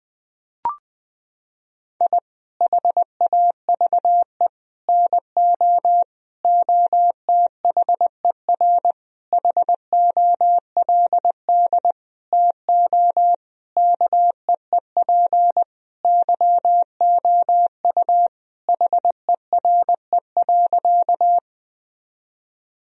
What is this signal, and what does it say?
Morse code, heard as a keyed tone, about 20 wpm, 700 Hz. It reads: I HAVE NO OTHER HOLD TO KEEP YOU HERE.
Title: Sentences from Top 1000 Words - Review Part 2 - 20wpm